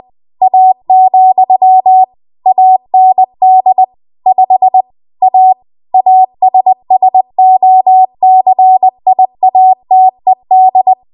Morse code, heard as a keyed tone, about 20 wpm, 750 Hz. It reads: A, AND 5 A ASSOCIATED
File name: Morse / 20WPM